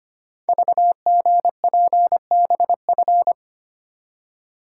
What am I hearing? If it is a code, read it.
Morse code, heard as a keyed tone, about 25 wpm, 700 Hz. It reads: VGPBF
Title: Code Groups - 5 Letters - 25wpm